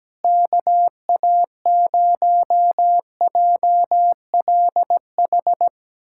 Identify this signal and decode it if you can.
Morse code, heard as a keyed tone, about 17 wpm, 700 Hz. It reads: KA0JLH
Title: USA Callsign DE Callsign - 6 Characters - 17wpm